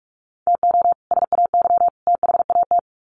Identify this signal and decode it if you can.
Morse code, heard as a keyed tone, about 44 wpm, 700 Hz. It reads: TO SAY THAT
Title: Sets of 3 Words 45wpm